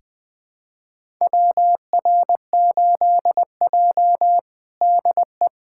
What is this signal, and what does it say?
Morse code, heard as a keyed tone, about 21 wpm, 700 Hz. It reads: WR8J DE
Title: USA Callsign DE Callsign - 4 Characters - 20wpm